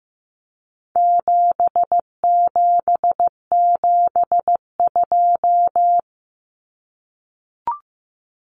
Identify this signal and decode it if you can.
Morse code, heard as a keyed tone, about 15 wpm, 700 Hz. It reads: 7772
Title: Numbers - 4-Digit 15wpm